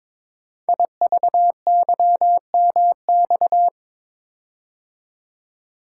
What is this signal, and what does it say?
Morse code, heard as a keyed tone, about 23 wpm, 700 Hz. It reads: IVYMX
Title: Code Groups - 5 Letters - 22wpm